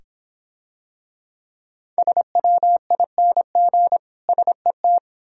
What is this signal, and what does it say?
Morse code, heard as a keyed tone, about 26 wpm, 700 Hz. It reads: SWING SET